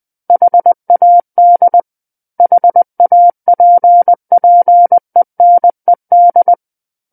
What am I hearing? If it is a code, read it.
Morse code, heard as a keyed tone, about 20 wpm, 700 Hz. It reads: HAD HAPPENED
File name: 20 WPM